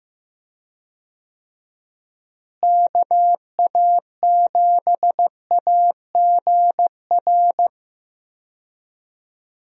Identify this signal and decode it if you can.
Morse code, heard as a keyed tone, about 15 wpm, 700 Hz. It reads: KA7AGR
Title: USA Call Signs - 6 Characters - Repeated 2x - 15wpm